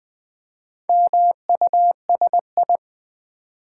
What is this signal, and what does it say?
Morse code, heard as a keyed tone, about 20 wpm, 700 Hz. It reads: MUSI
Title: Build A Word - 2 Letters At A Time - 8-Letter Words - 20wpm